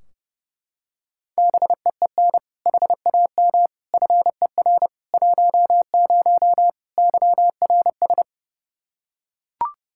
Morse code, heard as a keyed tone, about 30 wpm, 700 Hz. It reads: BEEN HAM FER 10 YRS
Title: Practice QSO - Exchange 3 - More Info - 30wpm